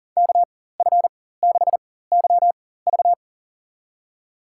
Morse code, heard as a keyed tone, about 40 wpm, spaced as for 19 wpm, 700 Hz. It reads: KF6YV
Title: USA Call Signs - 5 Characters - 4x Character Spacing - 40wpm